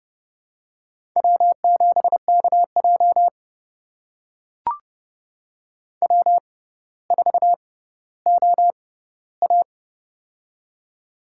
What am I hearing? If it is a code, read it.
Morse code, heard as a keyed tone, about 30 wpm, spaced as for 15 wpm, 700 Hz. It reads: W7KJ W4OA
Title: USA Call Signs - 4 Characters - 6x Character Spacing - 30wpm